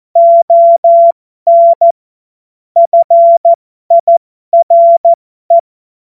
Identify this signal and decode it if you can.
Morse code, heard as a keyed tone, about 14 wpm, spaced as for 12 wpm, 680 Hz. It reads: ON FIRE
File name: Morsekurs / Witze